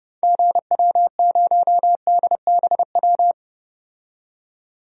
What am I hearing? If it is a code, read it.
Morse code, heard as a keyed tone, about 30 wpm, 700 Hz. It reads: GW0DBW
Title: DXCC-100 Call Signs - 6 Characters - 30wpm